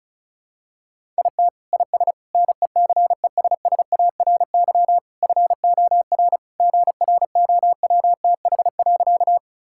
Morse code, heard as a keyed tone, about 36 wpm, 700 Hz. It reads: IT IS NECESSARY FOR GROWTH.
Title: Sentences from Top 1000 Words - Review Part 5 - 35wpm